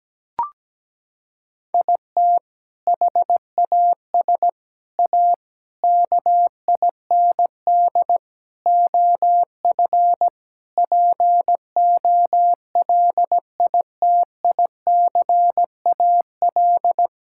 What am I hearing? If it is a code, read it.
Morse code, heard as a keyed tone, about 17 wpm, 700 Hz. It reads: IT HAS A KIND OF POLITICAL
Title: Sentences from Top 1000 Words - Review Part 5 - 17wpm